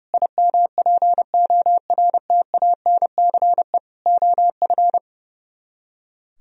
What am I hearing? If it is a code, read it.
Morse code, heard as a keyed tone, about 39 wpm, 700 Hz. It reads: IMPORTANCE OF